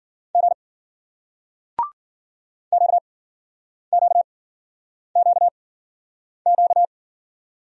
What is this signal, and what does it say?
Morse code, heard as a keyed tone, about 56 wpm, spaced as for 9 wpm, 700 Hz. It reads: G QQQQ